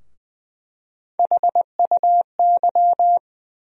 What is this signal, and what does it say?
Morse code, heard as a keyed tone, about 20 wpm, 700 Hz. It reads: HUY
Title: USA Call Sign Postfixes - 3 Letters 20wpm